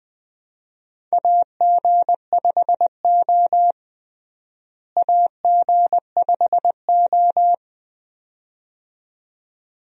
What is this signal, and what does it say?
Morse code, heard as a keyed tone, about 20 wpm, 700 Hz. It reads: AG5O AG5O
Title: USA Call Signs - 4 Characters - Repeated 2x - 20wpm